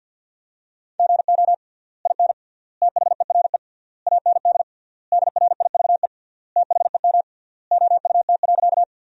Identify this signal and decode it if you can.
Morse code, heard as a keyed tone, about 53 wpm, 700 Hz. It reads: GO IN THERE AND DRIVE THEM OUT.
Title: 3x Word-Spacing - Sentences from Top 500 Words - 50wpm